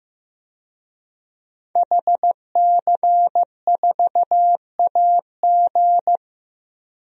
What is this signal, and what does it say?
Morse code, heard as a keyed tone, about 15 wpm, 700 Hz. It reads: HC4AG
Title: DXCC-100 Call Signs - 5 Characters - Repeated 2x - 15wpm